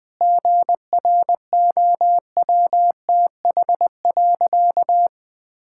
Morse code, heard as a keyed tone, about 21 wpm, 700 Hz. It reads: GROWTH.
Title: Sentences from Top 1000 Words - Review Part 5 - 20wpm